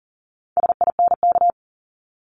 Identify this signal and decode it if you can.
Morse code, heard as a keyed tone, about 39 wpm, 700 Hz. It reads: SINK
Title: Multi-letter Phonemes 40wpm